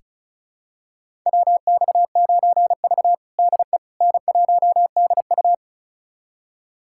Morse code, heard as a keyed tone, about 36 wpm, 700 Hz. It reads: WX9V DE N1DU
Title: USA Callsign DE Callsign - 4 Characters - 35wpm